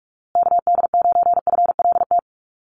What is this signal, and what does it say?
Morse code, heard as a keyed tone, about 44 wpm, 700 Hz. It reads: KD9FLT
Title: USA Call Signs Repeated 3x 45wpm